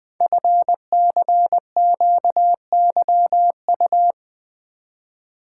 Morse code, heard as a keyed tone, about 20 wpm, 700 Hz. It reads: FCQYU
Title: Code Groups - 5 Letters - 20wpm